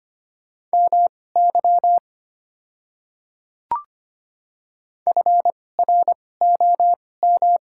Morse code, heard as a keyed tone, about 25 wpm, spaced as for 18 wpm, 700 Hz. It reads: MY FROM
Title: 2x Character Spacing - Top 100 Words - 25wpm